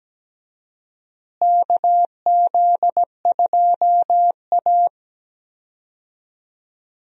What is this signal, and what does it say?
Morse code, heard as a keyed tone, about 17 wpm, 700 Hz. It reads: KZ2A
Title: USA Call Signs - 4 Characters - Repeated 2x - 17wpm